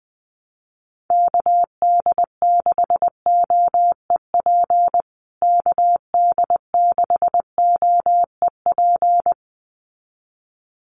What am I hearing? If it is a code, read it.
Morse code, heard as a keyed tone, about 19 wpm, 700 Hz. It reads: KD6OEP KD6OEP
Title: USA Call Signs Repeated 2x 20wpm